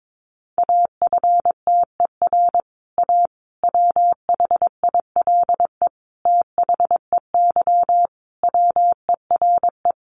Morse code, heard as a keyed tone, about 22 wpm, 700 Hz. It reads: AFTER A WHILE THEY WERE
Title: Sentences from Top 1000 Words 22wpm